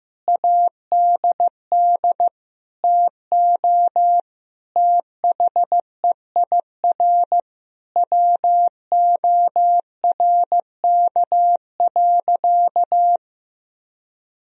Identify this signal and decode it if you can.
Morse code, heard as a keyed tone, about 15 wpm, 700 Hz. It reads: ADD TO THEIR WORK.
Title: Sentences from Top 1000 Words - Review Part 1 - 15wpm